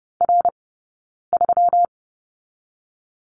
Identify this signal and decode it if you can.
Morse code, heard as a keyed tone, about 30 wpm, 700 Hz. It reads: R 3
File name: Lesson 20d - Number 3 - Two Characters 30wpm